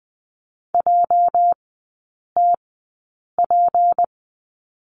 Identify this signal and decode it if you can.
Morse code, heard as a keyed tone, about 20 wpm, spaced as for 8 wpm, 700 Hz. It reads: JTP
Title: 205 Lesson 33e - Letter j - Three Characters 20wpm